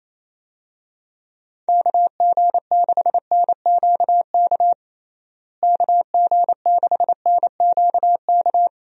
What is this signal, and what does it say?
Morse code, heard as a keyed tone, about 28 wpm, 700 Hz. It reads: KG6NQK KG6NQK
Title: USA Call Signs - 6 Characters - Repeated 2x - 28wpm